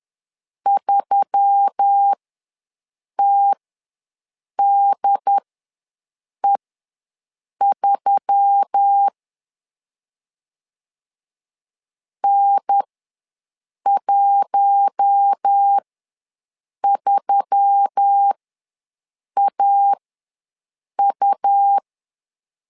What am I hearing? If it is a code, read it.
Morse code, heard as a keyed tone, about 10 wpm, spaced as for 6 wpm, 800 Hz. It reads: 3TDE3 N13AU
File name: LEZIONI / 4 Q1S3 ws 10wpm 2437